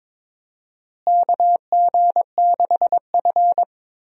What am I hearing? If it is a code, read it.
Morse code, heard as a keyed tone, about 22 wpm, 700 Hz. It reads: KG6F
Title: USA Call Signs - 4 Characters - Repeated 2x - 22wpm